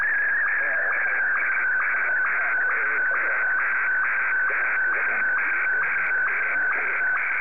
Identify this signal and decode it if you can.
SSTV (slow-scan television), Martin 1, with speech.
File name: sstv / M1-VK3